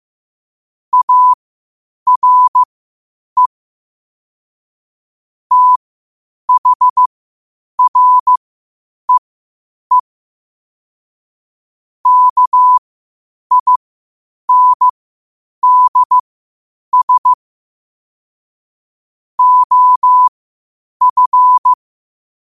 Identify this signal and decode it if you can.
Morse code, heard as a keyed tone, about 15 wpm, spaced as for 8 wpm, 1000 Hz. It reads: ARE THREE KINDS OF
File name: qotdfiles / 8wpm